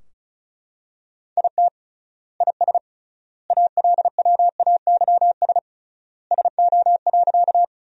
Morse code, heard as a keyed tone, about 36 wpm, 700 Hz. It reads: IT IS ALWAYS SO.
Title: 3x Word-Spacing - Sentences from Top 300 Words - 35wpm